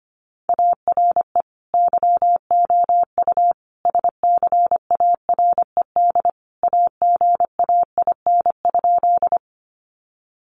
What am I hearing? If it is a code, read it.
Morse code, heard as a keyed tone, about 25 wpm, 700 Hz. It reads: ARE YOU SCARED AGAIN?
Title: Sentences from Top 700 Words 25wpm